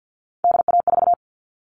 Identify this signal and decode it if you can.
Morse code, heard as a keyed tone, about 48 wpm, 700 Hz. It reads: DA4
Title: All Cut Numbers - 3-Digit 50wpm